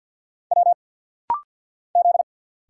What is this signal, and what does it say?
Morse code, heard as a keyed tone, about 50 wpm, 700 Hz. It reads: W Z
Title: Single Letters - Rapid-Fire 50wpm